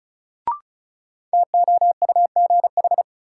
Morse code, heard as a keyed tone, about 36 wpm, 700 Hz. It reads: TOUGH
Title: Sentences from Top 1200 Words - New Word Drill Alt - 35wpm